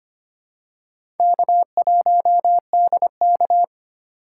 Morse code, heard as a keyed tone, about 25 wpm, 700 Hz. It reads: K1DK
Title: USA Call Signs - 4 Characters - 2x Character Spacing - 25wpm